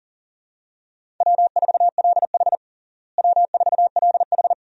Morse code, heard as a keyed tone, about 40 wpm, 700 Hz. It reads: W4LH W4LH
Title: USA Call Signs - 4 Characters - Repeated 2x - 40wpm